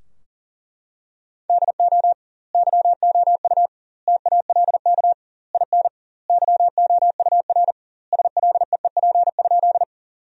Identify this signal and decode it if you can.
Morse code, heard as a keyed tone, about 38 wpm, 700 Hz. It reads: DO YOU TALK IN YOUR SLEEP?
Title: 2x Word-Spacing - Sentences from Top 500 Words - 40wpm